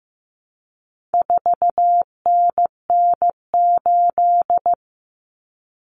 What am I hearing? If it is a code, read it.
Morse code, heard as a keyed tone, about 15 wpm, 700 Hz. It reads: 4NN8
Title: Common Cut Numbers - 4-Digit 15wpm